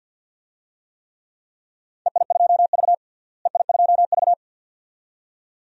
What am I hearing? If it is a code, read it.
Morse code, heard as a keyed tone, about 50 wpm, 700 Hz. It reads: EI2V EI2V
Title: DXCC-100 Call Signs - 4 Characters - Repeated 2x - 50wpm